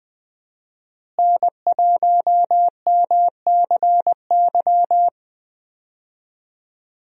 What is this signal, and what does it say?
Morse code, heard as a keyed tone, about 20 wpm, 700 Hz. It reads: N1MCY
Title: USA Call Signs - 5 Characters - Repeated 3x - 20wpm